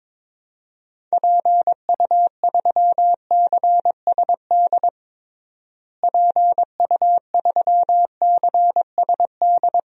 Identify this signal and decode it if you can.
Morse code, heard as a keyed tone, about 22 wpm, 700 Hz. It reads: PU3CSD PU3CSD
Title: DXCC-100 Call Signs - 6 Characters - Repeated 2x - 22wpm